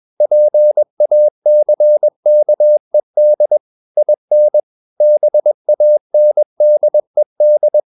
Morse code, heard as a keyed tone, about 21 wpm, 600 Hz. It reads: PACKED IN BANDED
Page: CW med Gnister